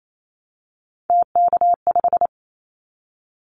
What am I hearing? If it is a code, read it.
Morse code, heard as a keyed tone, about 28 wpm, 700 Hz. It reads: TK5
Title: Callsign Prefixes for Top 100 DXCC - 28wpm